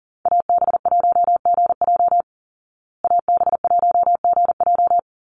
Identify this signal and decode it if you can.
Morse code, heard as a keyed tone, about 40 wpm, 700 Hz. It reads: AB1GJ AB1GJ
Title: Call Signs Repeated 3x - 40wpm